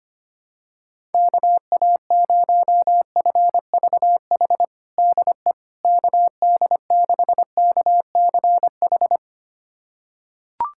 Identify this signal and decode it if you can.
Morse code, heard as a keyed tone, about 25 wpm, 700 Hz. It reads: KA0FVH DE KD6KCH
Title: USA Callsign DE Callsign - 6 Characters - 25wpm